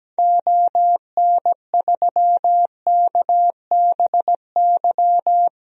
Morse code, heard as a keyed tone, about 17 wpm, 700 Hz. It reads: ON3KBY
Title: DXCC-100 Call Signs - 6 Characters - 17wpm